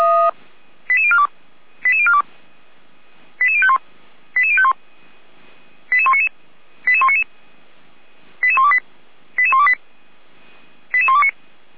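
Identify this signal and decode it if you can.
Five-tone selective call (ZVEI), heard as quick runs of five tones, tones of about 70 ms. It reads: 88042 88042 88051 88051 88199 88199 88128 88128 88129
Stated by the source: ZVEI five-tone call received on SDR, from feuerwehr-orlamuende.org fire brigade server.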